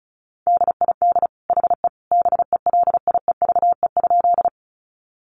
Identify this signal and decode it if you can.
Morse code, heard as a keyed tone, about 34 wpm, 700 Hz. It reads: DID HE BELIEVE?
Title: Sentences from Top 1000 Words 35wpm